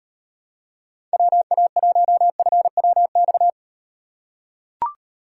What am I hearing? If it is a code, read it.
Morse code, heard as a keyed tone, about 38 wpm, 700 Hz. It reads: WA1FWX